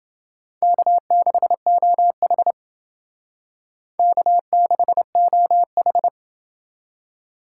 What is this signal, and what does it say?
Morse code, heard as a keyed tone, about 28 wpm, 700 Hz. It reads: K6OH K6OH